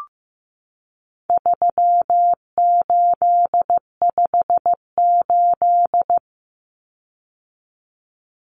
Morse code, heard as a keyed tone, about 15 wpm, 700 Hz. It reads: 3858
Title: Common Cut Numbers - 4-Digit 15wpm